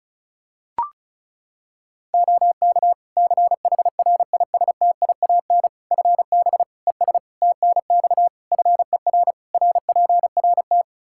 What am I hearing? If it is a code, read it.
Morse code, heard as a keyed tone, about 36 wpm, 700 Hz. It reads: OK CHRISTIAN FB ES TNX FER RPRT
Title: Practice QSO - Exchange 2 - Info - 35wpm